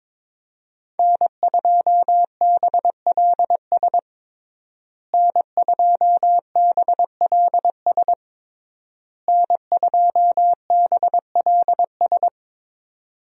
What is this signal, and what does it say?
Morse code, heard as a keyed tone, about 22 wpm, 700 Hz. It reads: N2BLS N2BLS N2BLS
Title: USA Call Signs - 5 Characters - Repeated 3x - 22wpm